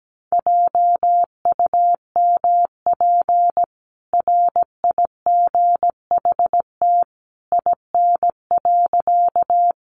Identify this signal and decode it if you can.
Morse code, heard as a keyed tone, about 17 wpm, 700 Hz. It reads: JUMP RIGHT IN.
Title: Sentences from Top 1300 Words 17wpm